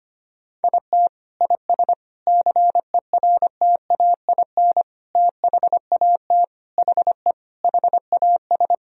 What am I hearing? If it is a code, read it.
Morse code, heard as a keyed tone, about 25 wpm, 700 Hz. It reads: IT IS CERTAIN THAT HE HAS
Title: Sentences from Top 1000 Words - Review Part 2 - 25wpm